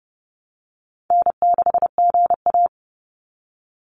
Morse code, heard as a keyed tone, about 30 wpm, 700 Hz. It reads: N6GA
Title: Easy Call Signs Repeated 3x - 30wpm